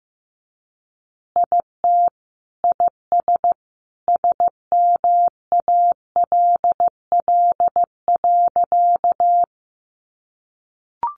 Morse code, heard as a keyed tone, about 15 wpm, 700 Hz. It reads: IT IS SMALL.
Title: Easy Phrases 15wpm